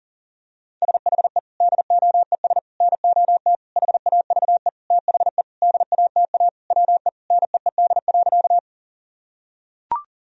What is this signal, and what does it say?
Morse code, heard as a keyed tone, about 37 wpm, 700 Hz. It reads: SHE DOES NOT HAVE THE DATA WE NEED.
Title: Sentences from Top 1000 Words - Review Part 3 - 40wpm